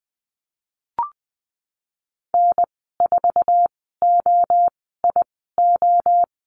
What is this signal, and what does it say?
Morse code, heard as a keyed tone, about 19 wpm, spaced as for 14 wpm, 700 Hz. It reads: N4OIO
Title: 088 Lesson 15g - Number 5 - Call Signs 20wpm